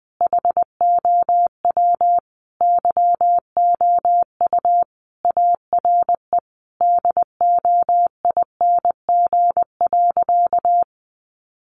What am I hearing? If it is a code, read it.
Morse code, heard as a keyed tone, about 20 wpm, 700 Hz. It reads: HOW YOU ARE DOING.
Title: Sentences from Top 1400 Words 20wpm